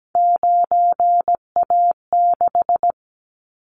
Morse code, heard as a keyed tone, about 17 wpm, 700 Hz. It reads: 9A6
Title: Callsign Prefixes for Top 50 DXCC - 17wpm